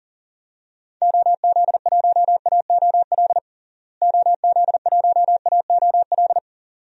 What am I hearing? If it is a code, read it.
Morse code, heard as a keyed tone, about 37 wpm, 700 Hz. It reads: OZ1AOL OZ1AOL
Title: DXCC-100 Call Signs - 6 Characters - Repeated 2x - 40wpm